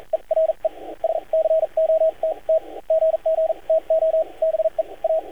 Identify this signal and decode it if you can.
Morse code, heard as a keyed tone, about 44 wpm, spaced as for 36 wpm, 640 Hz, with radio noise. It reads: E RE SCOTT GG TO BE A
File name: archived / CRSnet